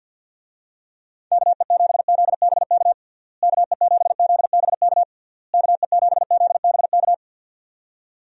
Morse code, heard as a keyed tone, about 52 wpm, 700 Hz. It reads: KE7ZBK KE7ZBK KE7ZBK
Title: USA Call Signs - 6 Characters - Repeated 3x - 50wpm